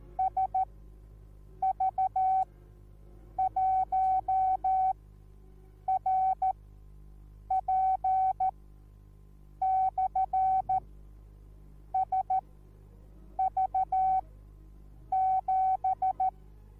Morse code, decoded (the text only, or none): SV1RP/SV7